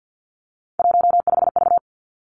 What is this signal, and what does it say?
Morse code, heard as a keyed tone, about 50 wpm, 700 Hz. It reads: 15V
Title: Three Challenging Letter-Number Combinations 50wpm